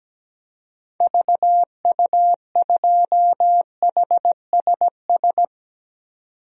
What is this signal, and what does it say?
Morse code, heard as a keyed tone, about 17 wpm, 700 Hz. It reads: VU2HSS
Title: DXCC-100 Call Signs - 6 Characters - 17wpm